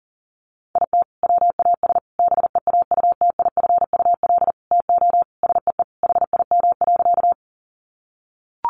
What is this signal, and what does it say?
Morse code, heard as a keyed tone, about 40 wpm, 700 Hz. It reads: IT WAS BEAUTIFUL TO SEE HIM.
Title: Sentences from Top 900 Words 40wpm